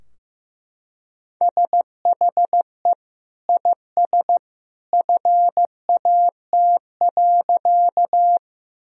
Morse code, heard as a keyed tone, about 15 wpm, 700 Hz. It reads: SHE IS FAT.
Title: Sentences from Top 1000 Words - Review Part 4 - 15wpm